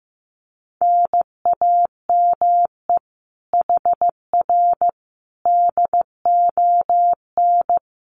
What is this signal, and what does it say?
Morse code, heard as a keyed tone, about 15 wpm, 700 Hz. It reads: NAME HR DON
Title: Common QSO Phrases Encore 15wpm